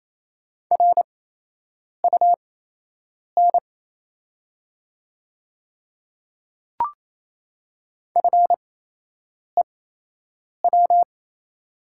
Morse code, heard as a keyed tone, about 28 wpm, spaced as for 8 wpm, 700 Hz. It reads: RUN FEW
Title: 8x Character Spacing - Top 300 Words - 28wpm